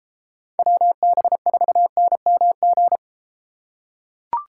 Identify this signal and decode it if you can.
Morse code, heard as a keyed tone, about 34 wpm, 700 Hz. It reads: WB4NMG